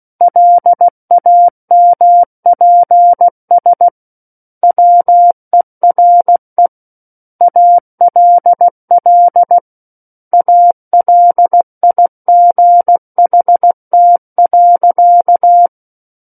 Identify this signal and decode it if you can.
Morse code, heard as a keyed tone, about 16 wpm, 700 Hz. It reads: LAMPS WERE ALL ALIGHT.
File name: Book One / 07-Chapter 7 - 16 WPM